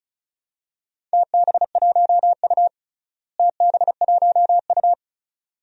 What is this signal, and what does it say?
Morse code, heard as a keyed tone, about 36 wpm, 700 Hz. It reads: TB1U TB1U
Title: DXCC-100 Call Signs - 4 Characters - Repeated 2x - 35wpm